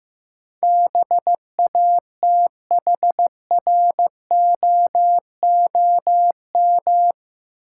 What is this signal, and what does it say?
Morse code, heard as a keyed tone, about 15 wpm, 700 Hz. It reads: BATHROOM
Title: Sets of 5 Words Encore - New Word Drill - 15wpm